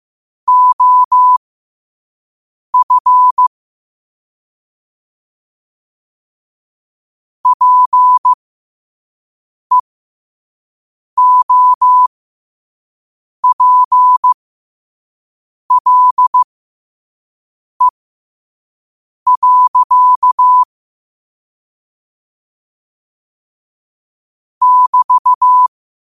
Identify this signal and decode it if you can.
Morse code, heard as a keyed tone, about 15 wpm, spaced as for 5 wpm, 1000 Hz. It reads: OF PEOPLE. <BT>
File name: qotdfiles / 5wpm